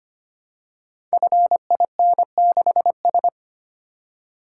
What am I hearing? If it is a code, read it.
Morse code, heard as a keyed tone, about 25 wpm, 690 Hz. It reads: FIN6S
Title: Code Groups - 5 Letter:Numbers - 25wpm